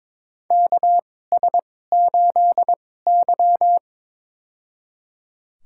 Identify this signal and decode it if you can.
Morse code, heard as a keyed tone, about 22 wpm, spaced as for 16 wpm, 700 Hz. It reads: KS8Y
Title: USA Call Signs - 4 Characters - 2x Character Spacing - 22wpm